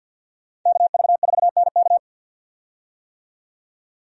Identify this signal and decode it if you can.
Morse code, heard as a keyed tone, about 51 wpm, 700 Hz. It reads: KV4NK
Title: USA Call Signs - 5 Characters - 50wpm